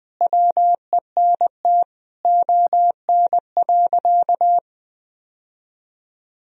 Morse code, heard as a keyed tone, about 20 wpm, 700 Hz. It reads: WENT ON.
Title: Sentences from Top 1000 Words - Review Part 1 - 20wpm